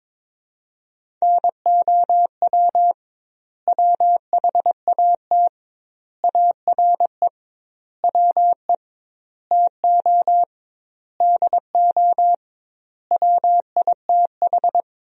Morse code, heard as a keyed tone, about 22 wpm, 700 Hz. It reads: NOW WHAT ARE WE TO DO WITH
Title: 2x Word-Spacing - Sentences from Top 100 Words - 22wpm